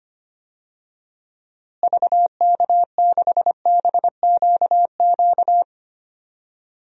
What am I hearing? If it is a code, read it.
Morse code, heard as a keyed tone, about 25 wpm, 700 Hz. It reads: VK6BQQ
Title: DXCC-100 Call Signs - 6 Characters - Repeated 2x - 25wpm